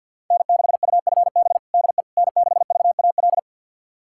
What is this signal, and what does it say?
Morse code, heard as a keyed tone, about 47 wpm, 700 Hz. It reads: N6UFB DE N6VAL
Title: USA Callsign DE Callsign - 5 Characters - 50wpm